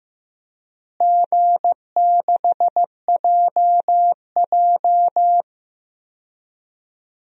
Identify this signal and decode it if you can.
Morse code, heard as a keyed tone, about 15 wpm, 700 Hz. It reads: G6JJ
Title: DXCC-100 Call Signs - 4 Characters - Repeated 2x - 15wpm